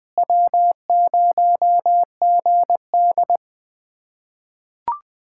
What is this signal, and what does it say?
Morse code, heard as a keyed tone, about 20 wpm, 700 Hz. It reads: W0GD
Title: USA Callsign DE Callsign - 4 Characters - 20wpm